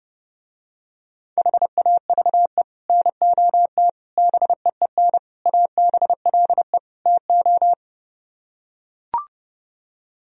Morse code, heard as a keyed tone, about 30 wpm, 700 Hz. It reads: HAVE NOT BEEN ABLE TO